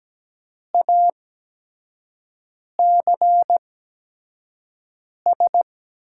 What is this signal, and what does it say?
Morse code, heard as a keyed tone, about 17 wpm, spaced as for 5 wpm, 700 Hz. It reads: ACS
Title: USA Call Signs - 5 Characters - 8x Character Spacing - 17wpm